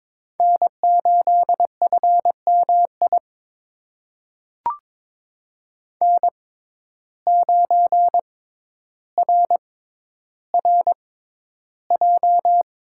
Morse code, heard as a keyed tone, about 22 wpm, spaced as for 11 wpm, 700 Hz. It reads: N8FMI N9RRJ
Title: USA Call Signs - 5 Characters - 6x Character Spacing - 22wpm